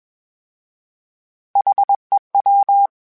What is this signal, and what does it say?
Morse code, heard as a keyed tone, about 21 wpm, 800 Hz. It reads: HEW